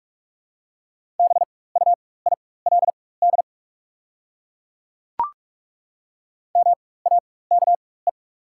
Morse code, heard as a keyed tone, about 43 wpm, spaced as for 21 wpm, 690 Hz. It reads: BUILD MAKE
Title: 4x Character Spacing - Top 200 Words - 45wpm